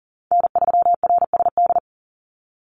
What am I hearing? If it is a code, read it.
Morse code, heard as a keyed tone, about 40 wpm, 700 Hz. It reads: N3RSD
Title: Call Signs Repeated 3x - 40wpm